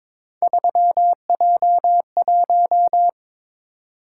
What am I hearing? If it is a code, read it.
Morse code, heard as a keyed tone, about 22 wpm, 700 Hz. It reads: 3J1
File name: practice / 22wpm / Three Letter-Number Combinations - 22wpm